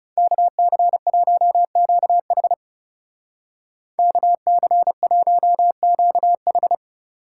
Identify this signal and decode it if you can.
Morse code, heard as a keyed tone, about 35 wpm, 700 Hz. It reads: KC1QH KC1QH
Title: Speed Racing - USA Call Signs 30wpm